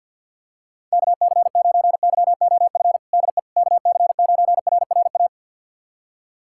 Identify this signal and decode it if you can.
Morse code, heard as a keyed tone, about 47 wpm, 690 Hz. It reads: KC9YOF DE KC9RRA
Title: USA Callsign DE Callsign 50wpm